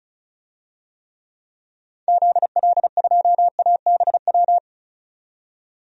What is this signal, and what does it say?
Morse code, heard as a keyed tone, about 34 wpm, 700 Hz. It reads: ZL2ABW
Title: DXCC-100 Call Signs - 6 Characters - 35wpm